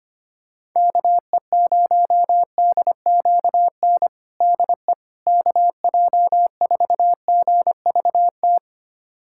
Morse code, heard as a keyed tone, about 25 wpm, 700 Hz. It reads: KE0DQN DE KJ4GVT
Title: USA Callsign DE Callsign - 6 Characters - 25wpm